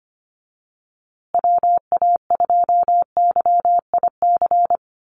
Morse code, heard as a keyed tone, about 25 wpm, 700 Hz. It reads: WA2YIC
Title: USA Call Signs Repeated 3x 25wpm